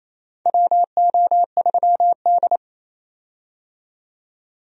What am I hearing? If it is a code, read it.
Morse code, heard as a keyed tone, about 28 wpm, 700 Hz. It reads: WO3D
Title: USA Call Signs - 4 Characters - 28wpm